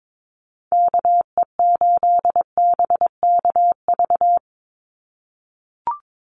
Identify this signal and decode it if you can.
Morse code, heard as a keyed tone, about 22 wpm, 700 Hz. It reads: KE8BKV
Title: USA Call Signs 22wpm